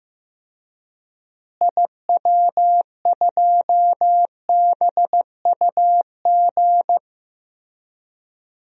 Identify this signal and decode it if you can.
Morse code, heard as a keyed tone, about 15 wpm, 700 Hz. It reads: IW2BUG
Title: DXCC-100 Call Signs - 6 Characters - Repeated 2x - 15wpm